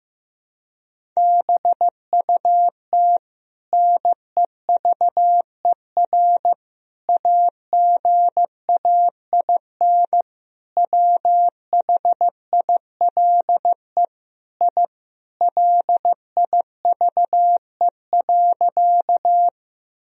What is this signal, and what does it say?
Morse code, heard as a keyed tone, about 15 wpm, 700 Hz. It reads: BUT NEVER AGAIN WHILE I LIVE.
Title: Sentences from Top 1000 Words - Review Part 1 - 15wpm